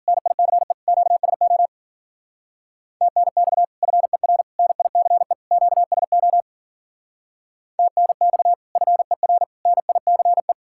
Morse code, heard as a keyed tone, about 52 wpm, 700 Hz. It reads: NICE QSO TNX FER NICE QSO TNX FER NICE